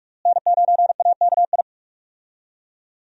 Morse code, heard as a keyed tone, about 42 wpm, 700 Hz. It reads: N9AKI
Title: USA Call Signs - 5 Characters - Repeated 3x - 45wpm